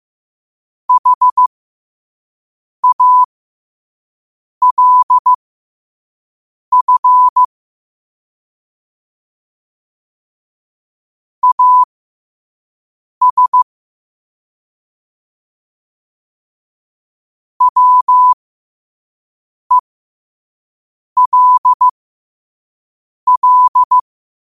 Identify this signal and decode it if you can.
Morse code, heard as a keyed tone, about 15 wpm, spaced as for 5 wpm, 1000 Hz. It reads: HALF AS WELL
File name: qotdfiles / 5wpm